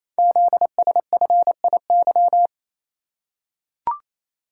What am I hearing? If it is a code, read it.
Morse code, heard as a keyed tone, about 28 wpm, 700 Hz. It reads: ZSFIY
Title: Code Groups - 5 Letters - 28wpm